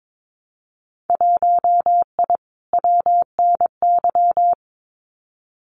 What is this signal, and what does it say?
Morse code, heard as a keyed tone, about 22 wpm, 700 Hz. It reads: 1I WNY
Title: Winter Field Day Exchanges Encore 22wpm